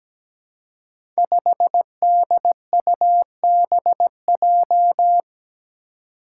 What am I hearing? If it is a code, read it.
Morse code, heard as a keyed tone, about 17 wpm, 700 Hz. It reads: 5DUBJ
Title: Five Challenging Letter-Number Combinations 17wpm